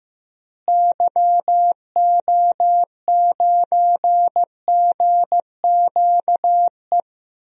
Morse code, heard as a keyed tone, about 15 wpm, 700 Hz. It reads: YO9GQE
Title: DXCC-100 Call Signs - 6 Characters - Repeated 2x - 15wpm